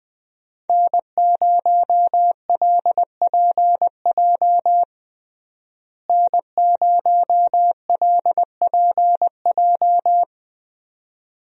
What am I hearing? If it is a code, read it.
Morse code, heard as a keyed tone, about 21 wpm, 700 Hz. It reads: N0LPJ N0LPJ
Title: USA Call Signs - 5 Characters - Repeated 2x - 20wpm